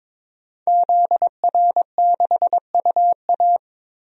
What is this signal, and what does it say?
Morse code, heard as a keyed tone, about 22 wpm, 700 Hz. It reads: ZR6UA
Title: DXCC-100 Call Signs - 5 Characters - 22wpm